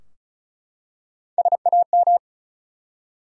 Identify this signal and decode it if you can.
Morse code, heard as a keyed tone, about 35 wpm, 700 Hz. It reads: SAM
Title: USA Common Names - Max 4 Letters - 35wpm